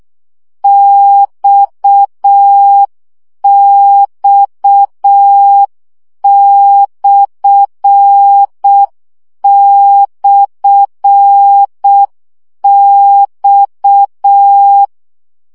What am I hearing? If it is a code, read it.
Morse code, heard as a keyed tone, about 6 wpm, 800 Hz. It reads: XX//X
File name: CW gyakorlas x per 5 baud